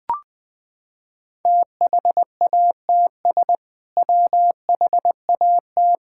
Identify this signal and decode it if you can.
Morse code, heard as a keyed tone, about 20 wpm, 700 Hz. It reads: THATS WHAT
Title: Sentences with 2-Word Phrases - 20wpm